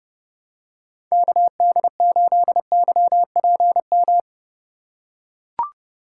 Morse code, heard as a keyed tone, about 29 wpm, 700 Hz. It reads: KD8YPM